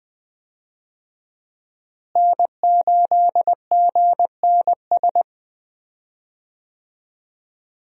Morse code, heard as a keyed tone, about 20 wpm, 700 Hz. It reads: N8GNS
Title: USA Call Signs - 5 Characters - Repeated 2x - 20wpm